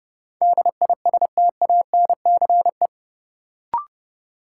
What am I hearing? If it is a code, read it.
Morse code, heard as a keyed tone, about 32 wpm, 700 Hz. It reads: DISTANCE
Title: Sentences from Top 1200 Words - New Word Drill Alt - 30wpm